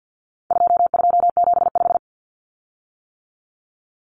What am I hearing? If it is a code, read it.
Morse code, heard as a keyed tone, about 48 wpm, 700 Hz. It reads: 2275
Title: Common Cut Numbers - 4-Digit 50wpm